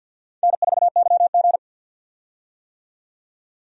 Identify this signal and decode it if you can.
Morse code, heard as a keyed tone, about 53 wpm, 700 Hz. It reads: N4YG
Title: USA Call Signs - 4 Characters - 50wpm